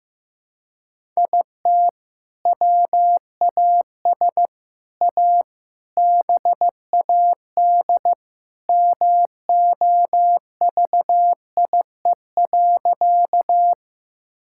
Morse code, heard as a keyed tone, about 15 wpm, 700 Hz. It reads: IT WAS A BAD MOVIE.
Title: Sentences from Top 1000 Words - Review Part 5 - 15wpm